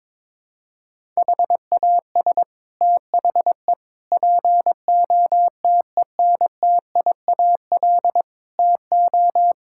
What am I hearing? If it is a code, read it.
Morse code, heard as a keyed tone, about 22 wpm, 700 Hz. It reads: HAS THE POTENTIAL TO
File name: practice / 22wpm / Sets of 6 Words - 22wpm